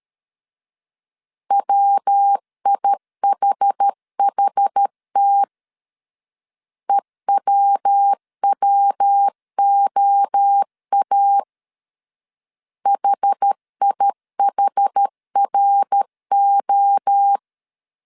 Morse code, heard as a keyed tone, about 13 wpm, 800 Hz. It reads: WIHHT EWWOA HIHRO
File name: CW / DET / 1 3 1500C S B 12wpm 2828